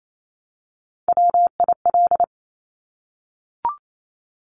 Morse code, heard as a keyed tone, about 28 wpm, 700 Hz. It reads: WIL
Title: Top Three Letter Combinations in Words 28wpm